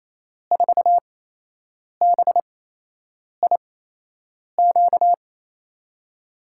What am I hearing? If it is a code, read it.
Morse code, heard as a keyed tone, about 28 wpm, spaced as for 8 wpm, 700 Hz. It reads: 4BIQ
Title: USA Call Signs - 8x Character Spacing - 28wpm